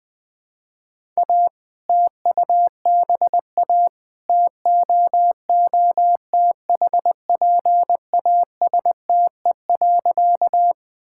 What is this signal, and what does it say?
Morse code, heard as a keyed tone, about 21 wpm, 700 Hz. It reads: A TUBA TOOTHPASTE.
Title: All-Age Appropriate Jokes - 20wpm